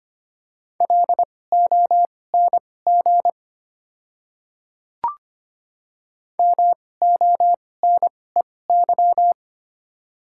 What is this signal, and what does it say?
Morse code, heard as a keyed tone, about 25 wpm, spaced as for 18 wpm, 700 Hz. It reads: LONG MONEY
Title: 2x Character Spacing - Top 500 Words - 25wpm